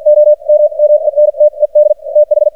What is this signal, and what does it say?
Morse code, heard as a keyed tone, about 52 wpm, spaced as for 42 wpm, 600 Hz. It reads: OMMENTED TH
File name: archived / CRSnet